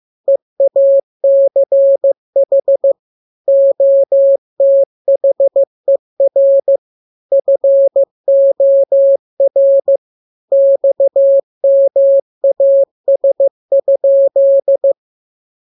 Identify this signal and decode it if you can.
Morse code, heard as a keyed tone, about 15 wpm, 550 Hz. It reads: EACH OTHER FOR XMAS?
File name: jokes / joke 17 15 0000